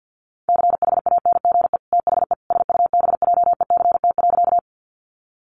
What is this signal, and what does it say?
Morse code, heard as a keyed tone, about 50 wpm, 700 Hz. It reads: CHANGE THE SUBJECT.
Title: Sentences from Top 800 Words 50wpm